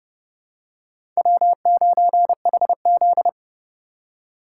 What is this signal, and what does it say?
Morse code, heard as a keyed tone, about 30 wpm, 700 Hz. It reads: W9HZ
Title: USA Call Signs - 4 Characters - 3x Character Spacing - 30wpm